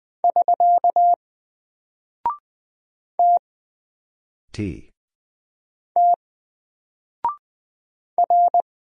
Morse code, heard as a keyed tone, about 20 wpm, 700 Hz. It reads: <SK> T T R
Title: Single Character (Letter:Number:Punctuation:Prosign) Alt 20wpm